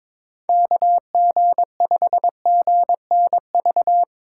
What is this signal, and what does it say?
Morse code, heard as a keyed tone, about 22 wpm, 700 Hz. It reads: KG5GNV
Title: USA Call Signs - 6 Characters - 2x Character Spacing - 22wpm